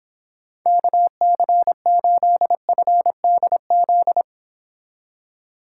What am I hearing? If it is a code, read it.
Morse code, heard as a keyed tone, about 25 wpm, 700 Hz. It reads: KC8FDZ